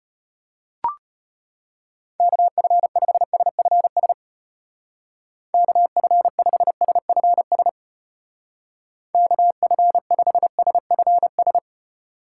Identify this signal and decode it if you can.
Morse code, heard as a keyed tone, about 31 wpm, 700 Hz. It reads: KF5SFS KF5SFS KF5SFS
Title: Speed Racing - USA 2x3 Call Signs 25wpm